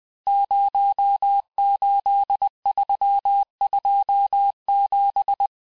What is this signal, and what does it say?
Morse code, heard as a keyed tone, about 20 wpm, 800 Hz. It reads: 08327
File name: cours-858-texte-chiffre-vitesse-19mpm